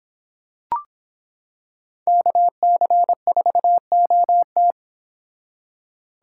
Morse code, heard as a keyed tone, about 26 wpm, 700 Hz. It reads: KC4OT